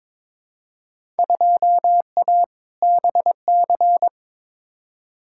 Morse code, heard as a keyed tone, about 22 wpm, 700 Hz. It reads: 2A BC
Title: Field Day Exchanges Encore - 22wpm